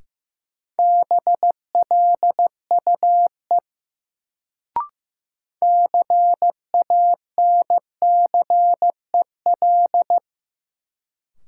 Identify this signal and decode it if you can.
Morse code, heard as a keyed tone, about 15 wpm, 700 Hz. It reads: BLUE CANCEL
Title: Sentences from Top 1200 Words - New Word Drill Alt - 15wpm